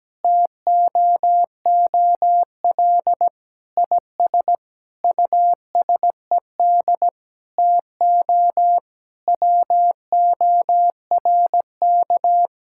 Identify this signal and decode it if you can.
Morse code, heard as a keyed tone, about 17 wpm, 700 Hz. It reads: TOOL IS USED TO WORK
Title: Sentences from Top 1000 Words - Review Part 6 - 17wpm